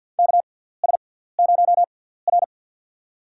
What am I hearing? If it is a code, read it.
Morse code, heard as a keyed tone, about 52 wpm, spaced as for 17 wpm, 700 Hz. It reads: KS0R
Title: USA Call Signs - 4 Characters - 6x Character Spacing - 50wpm